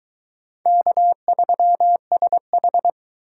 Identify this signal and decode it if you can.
Morse code, heard as a keyed tone, about 21 wpm, 700 Hz. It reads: K3SH